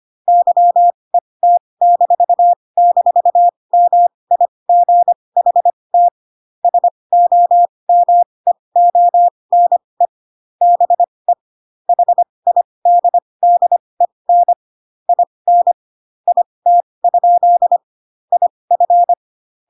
Morse code, heard as a keyed tone, about 25 wpm, spaced as for 20 wpm, 700 Hz. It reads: YET--MIGHT SOMEONE BE HIDDEN IN IT? IF
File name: Pinocchio / Pinocchio ch1 15 25 ik1hge